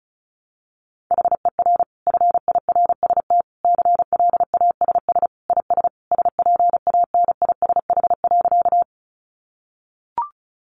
Morse code, heard as a keyed tone, about 34 wpm, 700 Hz. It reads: HER FIRST CLASS IS SPANISH.
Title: Additional Easy Phrases 35wpm